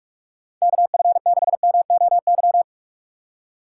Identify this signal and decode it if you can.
Morse code, heard as a keyed tone, about 41 wpm, 700 Hz. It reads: KF6MOY
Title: USA Call Signs - 6 Characters - 2x Character Spacing - 45wpm